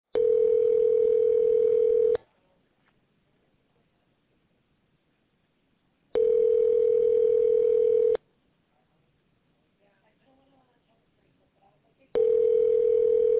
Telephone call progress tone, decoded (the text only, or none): ringback tone